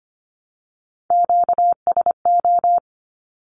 Morse code, heard as a keyed tone, about 25 wpm, 700 Hz. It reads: QSO
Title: Q-Codes Repeated 3x 25wpm